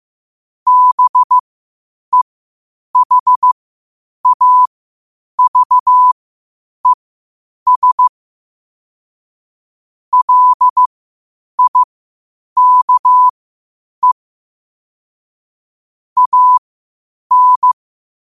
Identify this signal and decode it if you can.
Morse code, heard as a keyed tone, about 15 wpm, spaced as for 8 wpm, 1000 Hz. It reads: BEHAVES LIKE AN
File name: qotdfiles / 8wpm